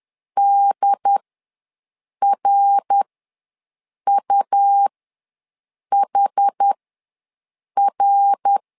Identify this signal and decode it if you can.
Morse code, heard as a keyed tone, about 11 wpm, spaced as for 6 wpm, 800 Hz. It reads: DRUHR
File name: LEZIONI / 3 MWRO / 3 MWR ws 10wpm 2151